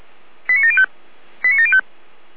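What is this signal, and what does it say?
Five-tone selective call (ZVEI), heard as quick runs of five tones, tones of about 70 ms. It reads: 89705 89705
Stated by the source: ZVEI five-tone call received by SDR.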